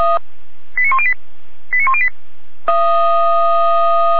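Five-tone selective call (ZVEI), heard as quick runs of five tones, tones of about 70 ms. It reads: 89108 89108
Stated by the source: Five-tone ZVEI selective call received by SDR, archived by feuerwehr-orlamuende.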